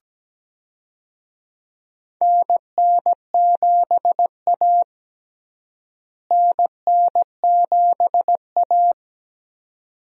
Morse code, heard as a keyed tone, about 17 wpm, 700 Hz. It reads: NN7A NN7A
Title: USA Call Signs - 4 Characters - Repeated 2x - 17wpm